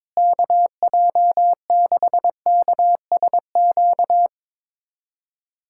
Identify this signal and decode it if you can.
Morse code, heard as a keyed tone, about 22 wpm, 700 Hz. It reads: KJ6KSQ
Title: USA Callsign DE Callsign 22wpm